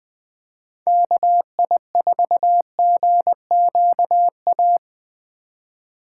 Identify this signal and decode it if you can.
Morse code, heard as a keyed tone, about 19 wpm, 700 Hz. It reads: KI4GQA